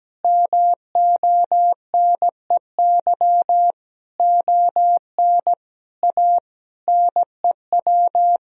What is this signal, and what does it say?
Morse code, heard as a keyed tone, about 17 wpm, 700 Hz. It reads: MONEY ON A NEW
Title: Sentences from Top 1000 Words - Review Part 4 - 17wpm